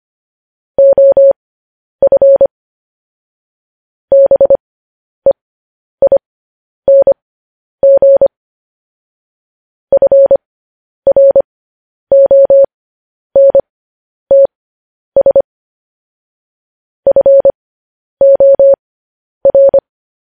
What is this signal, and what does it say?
Morse code, heard as a keyed tone, about 25 wpm, spaced as for 10 wpm, 550 Hz. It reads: OF BEING FRONTS FOR
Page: news headlines in morse code at 10 wpm